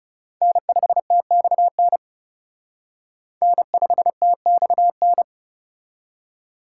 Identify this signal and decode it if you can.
Morse code, heard as a keyed tone, about 36 wpm, 700 Hz. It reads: N5TXN N5TXN
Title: Speed Racing - USA Call Signs 30wpm